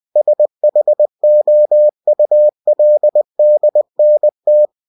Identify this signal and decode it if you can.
Morse code, heard as a keyed tone, about 20 wpm, 600 Hz. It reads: SHOULDNT
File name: cw-books / winnie-the-pooh / winnie-20wpm-page-0037